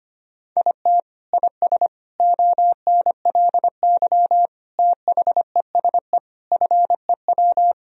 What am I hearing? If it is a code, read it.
Morse code, heard as a keyed tone, about 25 wpm, 700 Hz. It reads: IT IS ONLY THESE FEW
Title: Sentences from Top 1000 Words - Review Part 1 - 25wpm